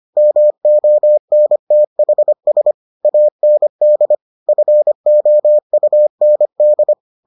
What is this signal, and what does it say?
Morse code, heard as a keyed tone, about 25 wpm, 600 Hz. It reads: MONTHS AND FOUND